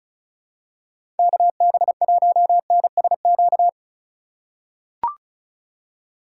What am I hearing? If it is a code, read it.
Morse code, heard as a keyed tone, about 36 wpm, 700 Hz. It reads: KB1NSQ
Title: USA Call Signs - 2x Character Spacing - 35wpm